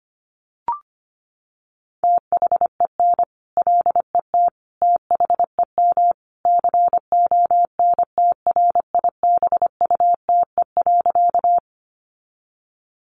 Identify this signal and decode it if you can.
Morse code, heard as a keyed tone, about 25 wpm, 700 Hz. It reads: THEN LET THEM CONTRIBUTE.
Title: Sentences from Top 1400 Words 25wpm